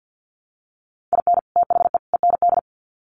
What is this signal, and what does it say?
Morse code, heard as a keyed tone, about 50 wpm, 700 Hz. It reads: IN THE END
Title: Sets of 3 Words 50wpm